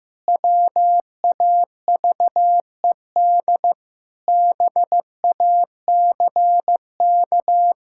Morse code, heard as a keyed tone, about 15 wpm, 700 Hz. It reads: WAVED BACK
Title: Pangram Challenge v2 - 15wpm